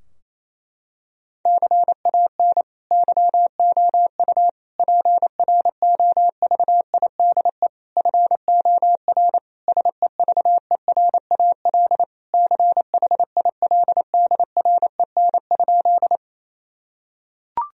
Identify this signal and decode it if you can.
Morse code, heard as a keyed tone, about 28 wpm, 700 Hz. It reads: CAN YOU PROVIDE FOR SEVERAL CHILDREN?
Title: Sentences from Top 1000 Words - Review Part 2 - 28wpm